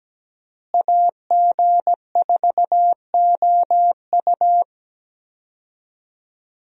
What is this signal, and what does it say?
Morse code, heard as a keyed tone, about 17 wpm, 700 Hz. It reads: AG4OU
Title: USA Call Signs - 5 Characters - Repeated 3x - 17wpm